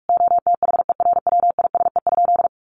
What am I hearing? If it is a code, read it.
Morse code, heard as a keyed tone, about 44 wpm, 700 Hz. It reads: OTHERWISE?
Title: Sentences from Top 1200 Words 45wpm